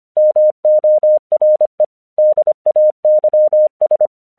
Morse code, heard as a keyed tone, about 25 wpm, 610 Hz. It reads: MORE DAYS
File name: 1 English 25wpm / AQC19